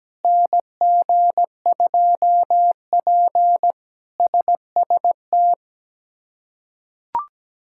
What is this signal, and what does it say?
Morse code, heard as a keyed tone, about 17 wpm, 700 Hz. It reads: NG2P SST
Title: K1USN SST QSO Practice - 17wpm